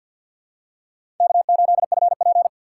Morse code, heard as a keyed tone, about 52 wpm, 700 Hz. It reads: K8FP
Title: USA Call Signs - 4 Characters - 2x Character Spacing - 50wpm